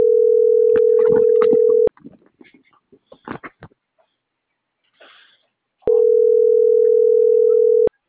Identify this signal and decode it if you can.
Ringback tone (North America), 440 + 480 Hz.